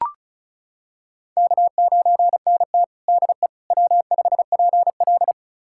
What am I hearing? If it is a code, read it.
Morse code, heard as a keyed tone, about 36 wpm, 700 Hz. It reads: K9NT DE W5PL
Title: USA Callsign DE Callsign - 4 Characters - 35wpm